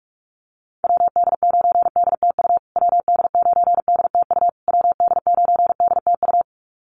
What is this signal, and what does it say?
Morse code, heard as a keyed tone, about 44 wpm, 700 Hz. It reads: WD9DTU WD9DTU WD9DTU
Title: USA Call Signs Repeated 3x 45wpm